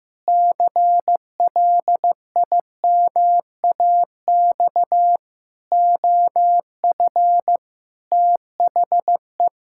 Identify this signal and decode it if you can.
Morse code, heard as a keyed tone, about 15 wpm, 700 Hz. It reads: CLIMAX OF THE
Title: Pangram Challenge v2 - 15wpm